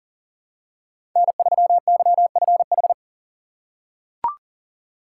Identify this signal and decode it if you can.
Morse code, heard as a keyed tone, about 40 wpm, 700 Hz. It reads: N3YFH
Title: USA Call Signs - 5 Characters - 2x Character Spacing - 40wpm